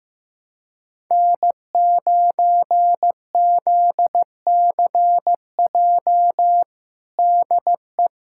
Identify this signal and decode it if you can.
Morse code, heard as a keyed tone, about 16 wpm, 700 Hz. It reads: N9ZCJ DE
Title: USA Callsign DE Callsign - 5 Characters - 15wpm